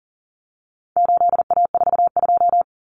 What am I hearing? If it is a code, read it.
Morse code, heard as a keyed tone, about 40 wpm, 700 Hz. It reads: 8A42
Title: Common Cut Numbers - 4-Digit 40wpm